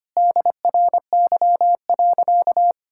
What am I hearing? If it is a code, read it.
Morse code, heard as a keyed tone, about 25 wpm, 700 Hz. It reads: DRY.
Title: Sentences from Top 1600 Words 25wpm